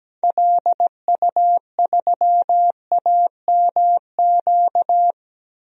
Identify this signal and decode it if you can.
Morse code, heard as a keyed tone, about 17 wpm, 700 Hz. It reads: LU3AMQ
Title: DXCC-100 Call Signs - 6 Characters - 17wpm